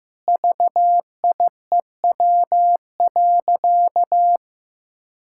Morse code, heard as a keyed tone, about 16 wpm, 700 Hz. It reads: VIEW.
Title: Sentences from Top 1000 Words - Review Part 6 - 15wpm